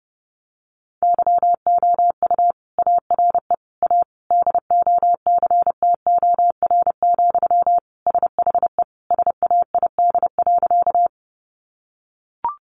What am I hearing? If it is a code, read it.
Morse code, heard as a keyed tone, about 30 wpm, 700 Hz. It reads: YOU ARE A DOCTOR, SHE SAID.
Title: Sentences from Top 1500 Words 30wpm